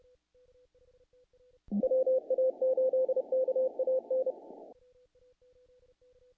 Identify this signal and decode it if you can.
Morse code, heard as a keyed tone, about 28 wpm, 520 Hz. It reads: WA8KAN
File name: contests / naqp-cw-k5tr-op-k5pi- -1800